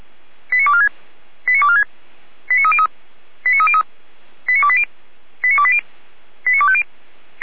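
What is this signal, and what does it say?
Five-tone selective call (ZVEI), heard as quick runs of five tones, tones of about 70 ms. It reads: 89246 89246 89303 89303 89280 89280 89250
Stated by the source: ZVEI five-tone selective call, code 89120, received by SDR.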